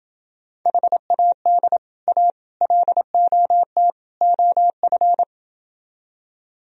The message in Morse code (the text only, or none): HAD A LOT OF